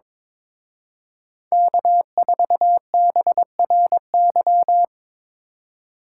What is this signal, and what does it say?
Morse code, heard as a keyed tone, about 26 wpm, 700 Hz. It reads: K4BRY
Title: Speed Racing - USA Call Signs 22wpm